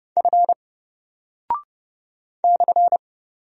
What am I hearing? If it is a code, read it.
Morse code, heard as a keyed tone, about 30 wpm, 700 Hz. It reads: F /
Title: Single Character (Letter:Number:Punctuation:Prosign) 30wpm